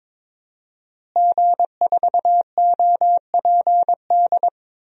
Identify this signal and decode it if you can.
Morse code, heard as a keyed tone, about 22 wpm, 700 Hz. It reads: G4OPD
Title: DXCC-100 Call Signs - 5 Characters - Repeated 3x - 22wpm